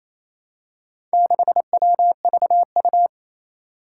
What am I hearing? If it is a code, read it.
Morse code, heard as a keyed tone, about 28 wpm, 700 Hz. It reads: 6WVU
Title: Four Challenging Letter-Number Combinations 28wpm